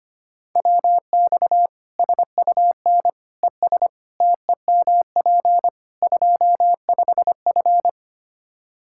Morse code, heard as a keyed tone, about 25 wpm, 700 Hz. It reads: WX SUN ES TEMP 25F
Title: Practice QSO - Exchange 2 - Info - 25wpm